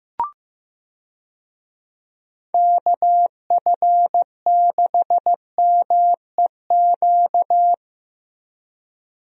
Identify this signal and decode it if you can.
Morse code, heard as a keyed tone, about 15 wpm, 700 Hz. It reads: KF6MEQ
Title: USA Call Signs - 6 Characters - Repeated 2x - 15wpm